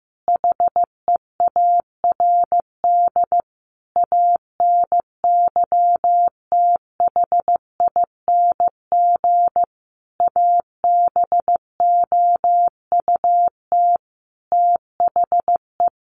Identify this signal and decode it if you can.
Morse code, heard as a keyed tone, about 15 wpm, 700 Hz. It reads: HEARD ANYTHING ABOUT THE
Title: Sentences from Top 1000 Words 15wpm